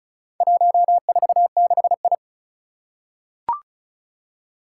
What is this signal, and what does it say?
Morse code, heard as a keyed tone, about 36 wpm, 700 Hz. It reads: 146I
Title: Four Challenging Letter-Number Combinations 35wpm